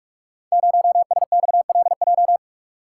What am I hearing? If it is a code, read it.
Morse code, heard as a keyed tone, about 44 wpm, 700 Hz. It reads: 0SXLJ
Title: Code Groups - 5 Letter:Numbers - 45wpm